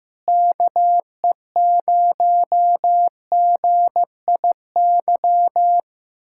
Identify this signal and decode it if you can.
Morse code, heard as a keyed tone, about 15 wpm, 700 Hz. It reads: KE0GIY
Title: Speed-Racing Common QSO Phrases Encore - 15wpm